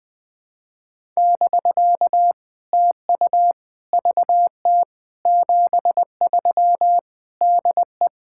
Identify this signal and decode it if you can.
Morse code, heard as a keyed tone, about 20 wpm, 700 Hz. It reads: <BK> TU VT 73 DE
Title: POTA QSO Practice - 20wpm